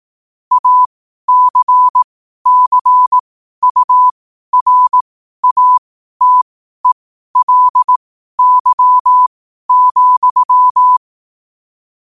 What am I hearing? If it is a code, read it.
Morse code, heard as a keyed tone, about 18 wpm, spaced as for 12 wpm, 1000 Hz. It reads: ACCURATELY,
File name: qotdfiles / 12wpm